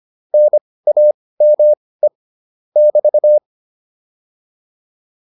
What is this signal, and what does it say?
Morse code, heard as a keyed tone, about 25 wpm, spaced as for 18 wpm, 600 Hz. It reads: NAME <BT>